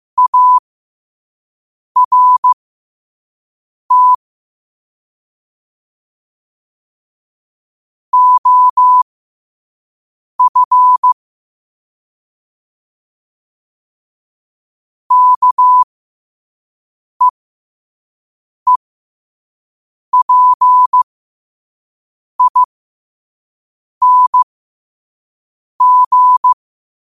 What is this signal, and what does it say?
Morse code, heard as a keyed tone, about 15 wpm, spaced as for 5 wpm, 1000 Hz. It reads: ART OF KEEPING
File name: qotdfiles / 5wpm